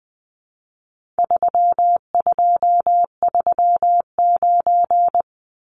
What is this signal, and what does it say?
Morse code, heard as a keyed tone, about 20 wpm, 700 Hz. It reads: 3239
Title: Numbers - 4-Digit 20wpm